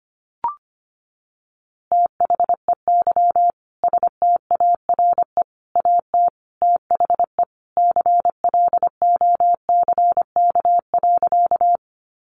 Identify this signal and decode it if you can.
Morse code, heard as a keyed tone, about 25 wpm, 700 Hz. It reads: THEY STARE AT THE CLOCK.
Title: Easy Phrases 25wpm